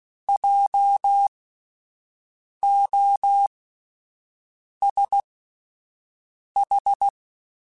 Morse code, heard as a keyed tone, about 16 wpm, spaced as for 5 wpm, 800 Hz. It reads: JOSH
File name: Spaced Alphas / Spaced Alphas mp3s / Js sp @10